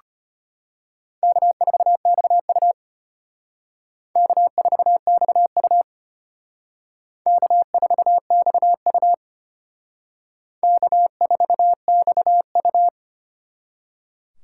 Morse code, heard as a keyed tone, about 32 wpm, 700 Hz. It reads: K4XU K4XU K4XU K4XU